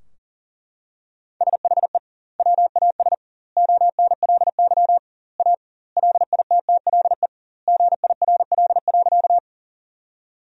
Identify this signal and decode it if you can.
Morse code, heard as a keyed tone, about 38 wpm, 700 Hz. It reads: SHE WAS ONLY A LITTLE GIRL.
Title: 2x Word-Spacing - Sentences from Top 400 Words - 40wpm